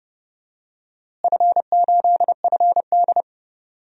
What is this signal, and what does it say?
Morse code, heard as a keyed tone, about 30 wpm, 700 Hz. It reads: F8FD
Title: DXCC-100 Call Signs - 4 Characters - Repeated 3x - 30wpm